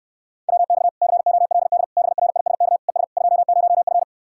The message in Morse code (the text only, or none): WX COLD BRISK ES 20F